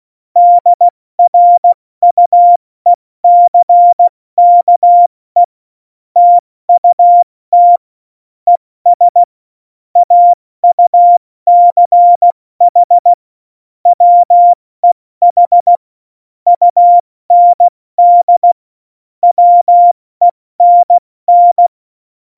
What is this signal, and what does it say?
Morse code, heard as a keyed tone, about 16 wpm, 700 Hz. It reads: DRUECKE TUT ES AUCH WEH UND WENN
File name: Morsekurs / Witze / Witz 09. Januar 2018, 70 F80